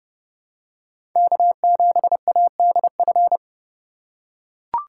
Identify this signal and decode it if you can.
Morse code, heard as a keyed tone, about 30 wpm, 700 Hz. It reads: K7ADF
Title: USA Call Signs - 5 Characters - 3x Character Spacing - 30wpm